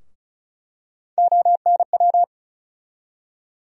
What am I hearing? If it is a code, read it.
Morse code, heard as a keyed tone, about 36 wpm, 700 Hz. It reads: ONW
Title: USA Call Sign Postfixes - 3 Letters 35wpm